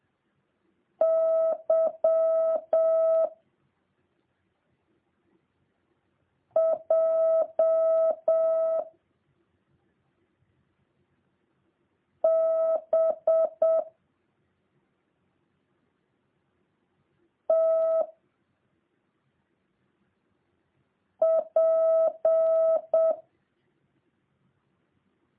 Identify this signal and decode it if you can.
Morse code, heard as a keyed tone, about 7 wpm, 650 Hz. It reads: Y J B T P